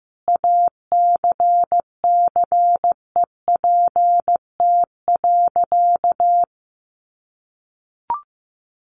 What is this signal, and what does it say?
Morse code, heard as a keyed tone, about 15 wpm, 700 Hz. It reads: ACCEPT.
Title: Sentences from Top 1300 Words 15wpm